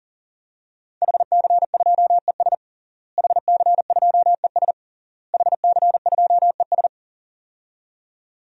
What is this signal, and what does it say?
Morse code, heard as a keyed tone, about 40 wpm, 700 Hz. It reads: HC2ES HC2ES HC2ES
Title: DXCC-100 Call Signs - 5 Characters - Repeated 3x - 40wpm